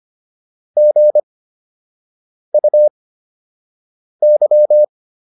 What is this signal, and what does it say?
Morse code, heard as a keyed tone, about 25 wpm, spaced as for 6 wpm, 600 Hz. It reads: GUY